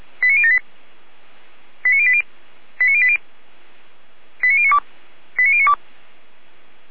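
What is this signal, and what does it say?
Five-tone selective call (ZVEI), heard as quick runs of five tones, tones of about 70 ms. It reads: 89078 89080 89080 89092 89092
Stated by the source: ZVEI five-tone selective call 89073, received by SDR, from the feuerwehr-orlamuende.org fire brigade audio server.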